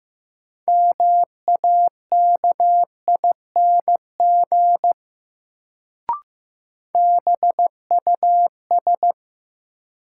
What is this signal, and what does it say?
Morse code, heard as a keyed tone, about 15 wpm, 700 Hz. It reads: MAKING BUS
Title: Sentences from Top 1000 Words - Review Part 6 - New Word Drill - 15wpm